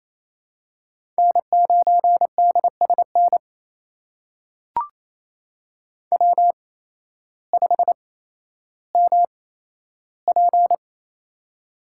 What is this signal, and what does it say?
Morse code, heard as a keyed tone, about 27 wpm, spaced as for 13 wpm, 700 Hz. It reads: N9DSN W5MP
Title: USA Call Signs - 8x Character Spacing - 28wpm